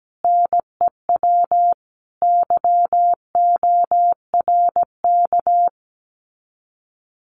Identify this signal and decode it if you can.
Morse code, heard as a keyed tone, about 17 wpm, 700 Hz. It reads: NEW YORK
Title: Common QSO Phrases Encore 17wpm